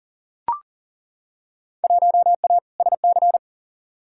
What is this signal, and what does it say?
Morse code, heard as a keyed tone, about 40 wpm, 700 Hz. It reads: 1A SC
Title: Field Day Exchanges - 40wpm